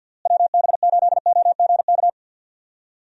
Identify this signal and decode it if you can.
Morse code, heard as a keyed tone, about 50 wpm, 700 Hz. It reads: WB8OGK
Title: USA Call Signs - 6 Characters - Repeated 3x - 50wpm